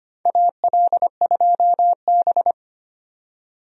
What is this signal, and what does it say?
Morse code, heard as a keyed tone, about 25 wpm, 700 Hz. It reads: AL2B
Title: USA Call Signs - 4 Characters - 25wpm